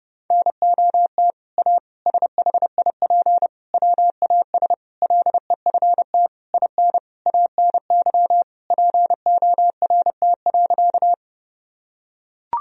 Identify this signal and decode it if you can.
Morse code, heard as a keyed tone, about 30 wpm, 700 Hz. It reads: NOT A SHIP WAS LEFT IN ANY PORT.
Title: Sentences from Top 1000 Words - Review Part 2 - 30wpm